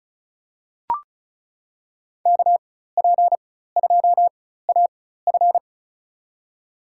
Morse code, heard as a keyed tone, about 36 wpm, spaced as for 16 wpm, 700 Hz. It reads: KP2AF
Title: USA Call Signs - 5 Characters - 4x Character Spacing - 35wpm